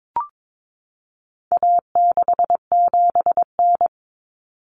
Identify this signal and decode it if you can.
Morse code, heard as a keyed tone, about 22 wpm, 700 Hz. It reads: A67N
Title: Common Cut Numbers - 4-Digit 22wpm